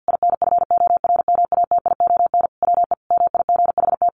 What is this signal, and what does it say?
Morse code, heard as a keyed tone, about 48 wpm, 700 Hz. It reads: INFORMATION WE MIGHT
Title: Sentences from Top 1300 Words 50wpm